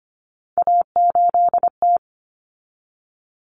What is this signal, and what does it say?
Morse code, heard as a keyed tone, about 25 wpm, 700 Hz. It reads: A8T
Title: Common Cut Numbers - 3-Digit 25wpm